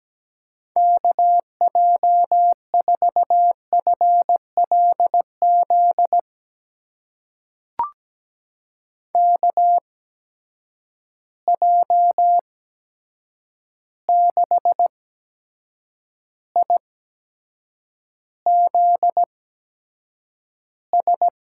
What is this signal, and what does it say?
Morse code, heard as a keyed tone, about 17 wpm, spaced as for 7 wpm, 700 Hz. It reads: KJ4FLZ KJ6IZS
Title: USA Call Signs - 6 Characters - 8x Character Spacing - 17wpm